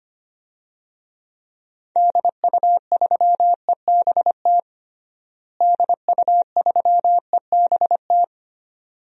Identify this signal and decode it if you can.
Morse code, heard as a keyed tone, about 25 wpm, 700 Hz. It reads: DU3EBT DU3EBT
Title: DXCC-100 Call Signs - 6 Characters - Repeated 2x - 25wpm